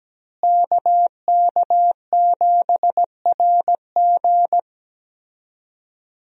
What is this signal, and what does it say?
Morse code, heard as a keyed tone, about 17 wpm, 700 Hz. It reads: KK7RG
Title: USA Call Signs - 5 Characters - Repeated 3x - 17wpm